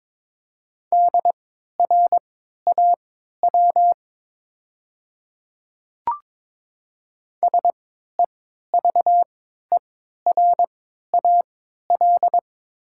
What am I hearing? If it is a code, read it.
Morse code, heard as a keyed tone, about 22 wpm, spaced as for 12 wpm, 700 Hz. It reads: DRAW SEVERAL
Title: 3x Character Spacing - Top 500 Words - 22wpm